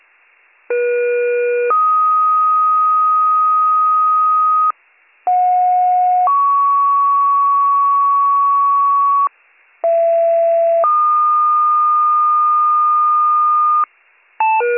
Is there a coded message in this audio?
Yes: two-tone page from 0.7 to 13.8 s, heard as a one-second tone, then a three-second tone, about 480 then 1250 Hz, about 730 then 1120 Hz, about 670 then 1220 Hz.